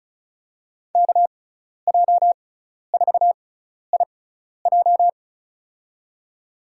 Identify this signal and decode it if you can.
Morse code, heard as a keyed tone, about 34 wpm, spaced as for 12 wpm, 700 Hz. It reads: KJ4IJ
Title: USA Call Signs - 5 Characters - 6x Character Spacing - 35wpm